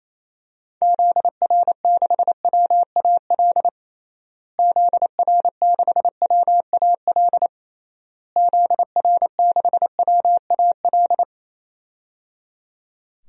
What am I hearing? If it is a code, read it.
Morse code, heard as a keyed tone, about 28 wpm, 700 Hz. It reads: ZR6WAL ZR6WAL ZR6WAL
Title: DXCC-100 Call Signs - 6 Characters - Repeated 3x - 28wpm